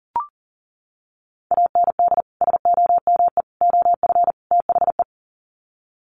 Morse code, heard as a keyed tone, about 40 wpm, 700 Hz. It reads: AND SOME OF THE
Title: Sets of 4 Words 40wpm